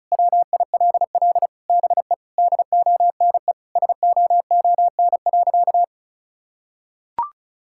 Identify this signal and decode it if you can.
Morse code, heard as a keyed tone, about 36 wpm, 700 Hz. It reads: WILL BE DONE SOON.
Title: Sentences from Top 1600 Words 35wpm